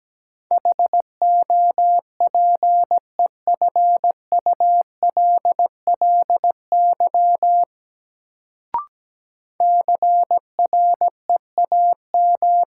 Morse code, heard as a keyed tone, about 17 wpm, 700 Hz. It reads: HOPEFULLY CREAM
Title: Sentences from Top 1500 Words - New Word Drill - 17wpm